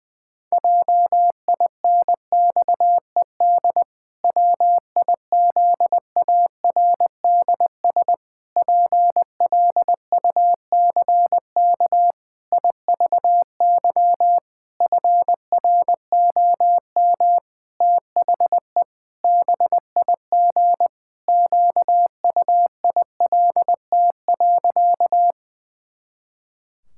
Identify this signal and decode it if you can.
Morse code, heard as a keyed tone, about 21 wpm, 700 Hz. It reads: JINXED WIZARDS PLUCK IVY FROM THE BIG QUILT.
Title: Pangram Challenge v2 - 20wpm